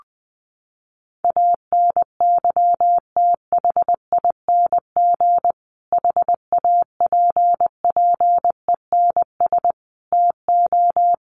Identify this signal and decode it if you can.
Morse code, heard as a keyed tone, about 20 wpm, 700 Hz. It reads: ANYTHING HAPPENS TO
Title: A Sound Of Thunder 20wpm